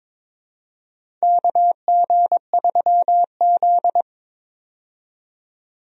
Morse code, heard as a keyed tone, about 22 wpm, 700 Hz. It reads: KG3Z
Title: USA Call Signs - 4 Characters - 22wpm